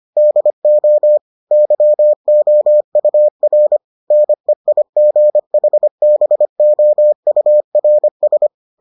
Morse code, heard as a keyed tone, about 25 wpm, 600 Hz. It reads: DO YOUR NEIGHBOURS